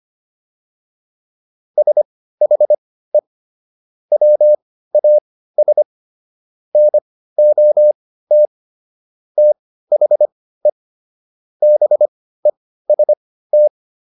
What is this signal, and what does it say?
Morse code, heard as a keyed tone, about 25 wpm, spaced as for 15 wpm, 600 Hz. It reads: SHE WAS NOT THE BEST